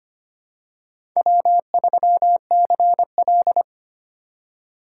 Morse code, heard as a keyed tone, about 25 wpm, 700 Hz. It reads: W3CL
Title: USA Call Signs - 4 Characters - 2x Character Spacing - 25wpm